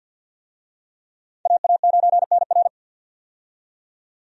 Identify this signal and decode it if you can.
Morse code, heard as a keyed tone, about 53 wpm, 700 Hz. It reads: AA9NR
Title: USA Call Signs - 5 Characters - 50wpm